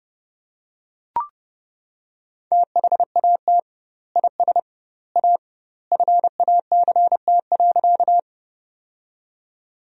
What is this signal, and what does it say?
Morse code, heard as a keyed tone, about 32 wpm, 700 Hz. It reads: THAT IS A FACT.
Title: 2x Word-Spacing - Sentences from Top 500 Words - 30wpm